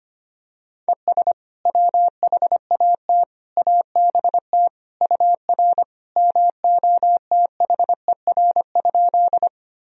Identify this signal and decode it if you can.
Morse code, heard as a keyed tone, about 25 wpm, 700 Hz. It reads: ES WHAT ABT UR MOTHER?
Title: Sentences from Top 200 Words and Abbreviations 25wpm